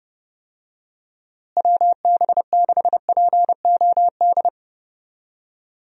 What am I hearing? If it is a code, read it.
Morse code, heard as a keyed tone, about 30 wpm, 700 Hz. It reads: WB6POD
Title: USA Call Signs - 6 Characters - Repeated 3x - 30wpm